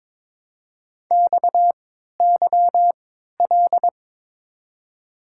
Morse code, heard as a keyed tone, about 22 wpm, spaced as for 12 wpm, 700 Hz. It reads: XYL
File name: Top 90 QSO Elements - 3x Character Spacing - 22wpm